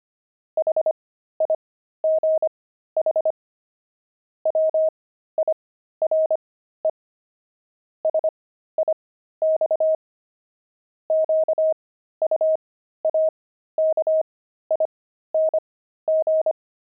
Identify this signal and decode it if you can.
Morse code, heard as a keyed tone, about 25 wpm, spaced as for 13 wpm, 630 Hz. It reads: HIGH WIRE SIX QUAKING
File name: pangram-fs-13-cs-25-jelly